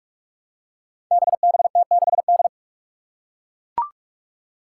Morse code, heard as a keyed tone, about 41 wpm, 700 Hz. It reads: BBT6D
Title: Five Challenging Letter-Number Combinations 45wpm